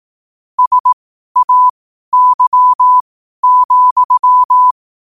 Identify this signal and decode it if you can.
Morse code, heard as a keyed tone, about 18 wpm, spaced as for 12 wpm, 1000 Hz. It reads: SAY,
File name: qotdfiles / 12wpm